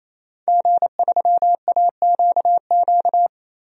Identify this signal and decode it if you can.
Morse code, heard as a keyed tone, about 28 wpm, 700 Hz. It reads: G3AQQ
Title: DXCC-100 Call Signs - 5 Characters - 28wpm